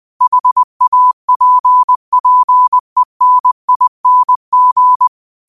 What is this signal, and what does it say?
Morse code, heard as a keyed tone, about 20 wpm, 1000 Hz. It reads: HAPPENING
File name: qotdfiles / 20wpm